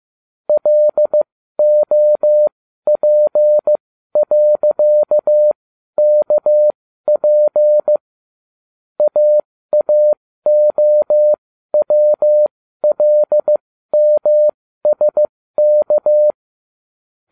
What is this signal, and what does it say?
Morse code, heard as a keyed tone, about 15 wpm, spaced as for 12 wpm, 600 Hz. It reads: LOP.KP AAOWLMSK
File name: cw / ubacw12 2